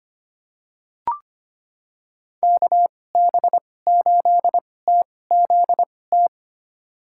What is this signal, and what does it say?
Morse code, heard as a keyed tone, about 25 wpm, spaced as for 18 wpm, 700 Hz. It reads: KB8TZT
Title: USA Call Signs - 6 Characters - 2x Character Spacing - 25wpm